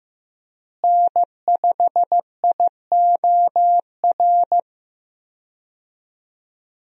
Morse code, heard as a keyed tone, about 15 wpm, 700 Hz. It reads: N5IOR
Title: USA Call Signs - 5 Characters - Repeated 3x - 15wpm